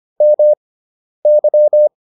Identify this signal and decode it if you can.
Morse code, heard as a keyed tone, about 25 wpm, spaced as for 10 wpm, 600 Hz. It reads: MY